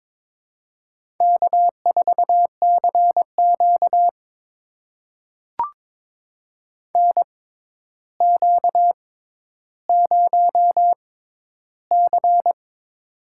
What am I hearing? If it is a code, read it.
Morse code, heard as a keyed tone, about 22 wpm, spaced as for 11 wpm, 700 Hz. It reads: K4CQ NQ0C
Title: USA Call Signs - 4 Characters - 6x Character Spacing - 22wpm